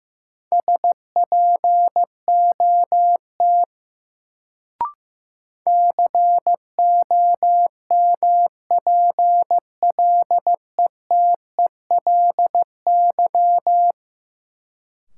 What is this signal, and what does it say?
Morse code, heard as a keyed tone, about 15 wpm, 700 Hz. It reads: SPOT COMPLETELY
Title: Sentences from Top 1100 Words - New Word Drill - 15wpm